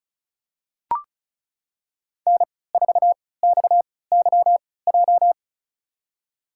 Morse code, heard as a keyed tone, about 36 wpm, spaced as for 20 wpm, 700 Hz. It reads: N4XYJ
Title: USA Call Signs - 5 Characters - 3x Character Spacing - 35wpm